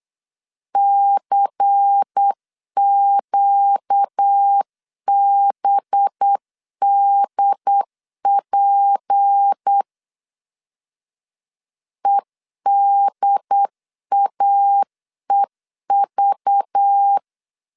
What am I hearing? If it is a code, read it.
Morse code, heard as a keyed tone, about 8 wpm, 800 Hz. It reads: CQBDP EDAEV
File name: CW / DET / 1 6 1500C S C 8wpm 5219